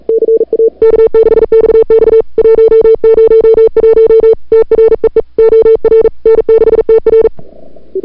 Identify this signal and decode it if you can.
Morse code, heard as a keyed tone, about 40 wpm, 440 Hz. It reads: CA K6XX 101 TREE OR N6TR
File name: CWSprint / February 2007 CW Sprint / 01-2007 CW Sprint Hour1